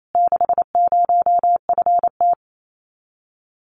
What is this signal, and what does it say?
Morse code, heard as a keyed tone, about 28 wpm, 700 Hz. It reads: 60FT
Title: Common QSO Phrases Encore 28wpm